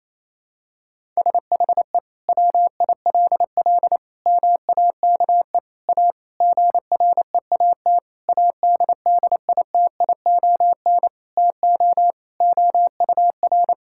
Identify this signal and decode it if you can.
Morse code, heard as a keyed tone, about 28 wpm, 700 Hz. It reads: SHE WILL MAKE A GREAT ADDITION TO OUR
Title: Sentences from Top 1000 Words - Review Part 6 - 28wpm